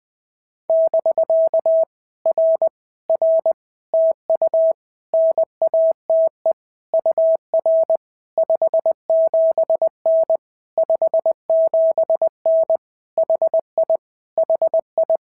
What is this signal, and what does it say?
Morse code, heard as a keyed tone, about 20 wpm, 650 Hz. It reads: <BK> R R TU NATE UR 57N 57N HI HI
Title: POTA QSO Practice - 20wpm